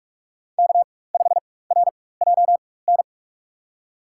Morse code, heard as a keyed tone, about 44 wpm, spaced as for 21 wpm, 690 Hz. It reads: K5RJN
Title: USA Call Signs - 4x Character Spacing - 45wpm